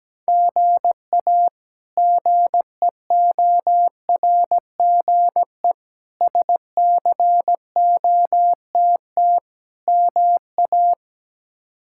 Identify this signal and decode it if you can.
Morse code, heard as a keyed tone, about 17 wpm, 700 Hz. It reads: GA GEORGE SCOTT MA
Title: K1USN SST QSO Practice - 17wpm